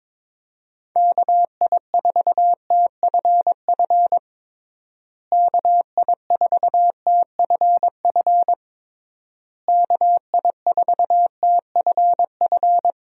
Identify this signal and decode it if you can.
Morse code, heard as a keyed tone, about 22 wpm, 700 Hz. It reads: KI4TFF KI4TFF KI4TFF
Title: USA Call Signs - 6 Characters - Repeated 3x - 22wpm